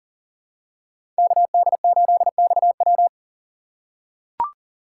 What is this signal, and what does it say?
Morse code, heard as a keyed tone, about 40 wpm, 700 Hz. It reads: KD8XW
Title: USA Call Signs - 5 Characters - 3x Character Spacing - 40wpm